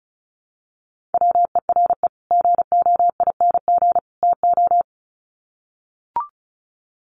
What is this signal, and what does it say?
Morse code, heard as a keyed tone, about 34 wpm, 700 Hz. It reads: WERE GOING TO
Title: Sets of 3 Words 35wpm